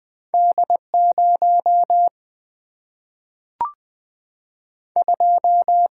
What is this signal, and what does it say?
Morse code, heard as a keyed tone, about 20 wpm, 700 Hz. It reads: D0 2
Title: Single-Letter Single-Number 20wpm